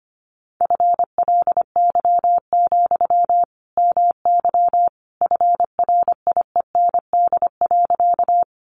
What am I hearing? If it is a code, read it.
Morse code, heard as a keyed tone, about 25 wpm, 700 Hz. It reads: FLY, MY FRIEND.
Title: Sentences from Top 500 Words 25wpm